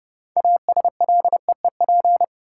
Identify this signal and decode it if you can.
Morse code, heard as a keyed tone, about 31 wpm, 700 Hz. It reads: ASLEEP
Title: Six Letter Words - 30wpm